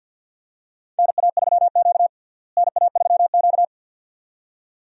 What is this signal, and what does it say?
Morse code, heard as a keyed tone, about 50 wpm, 700 Hz. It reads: NA3Q NA3Q
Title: USA Call Signs - 4 Characters - Repeated 2x - 50wpm